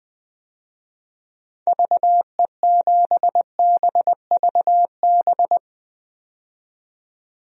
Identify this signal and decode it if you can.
Morse code, heard as a keyed tone, about 20 wpm, 700 Hz. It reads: VE7BVB
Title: DXCC-100 Call Signs - 6 Characters - 20wpm